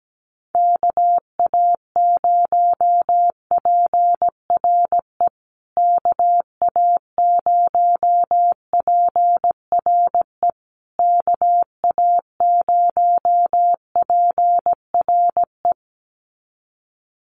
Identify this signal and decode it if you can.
Morse code, heard as a keyed tone, about 17 wpm, 700 Hz. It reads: KA0PRE KA0PRE KA0PRE
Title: USA Call Signs Repeated 3x 17wpm